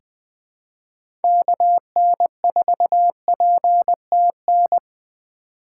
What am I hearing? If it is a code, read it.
Morse code, heard as a keyed tone, about 20 wpm, 700 Hz. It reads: KN4PTN
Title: USA Call Signs - 4x Character Spacing - 20wpm